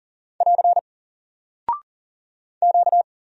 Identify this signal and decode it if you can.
Morse code, heard as a keyed tone, about 39 wpm, 700 Hz. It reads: <AR> Q
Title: Single Character (Letter:Number:Punctuation:Prosign) Alt 40wpm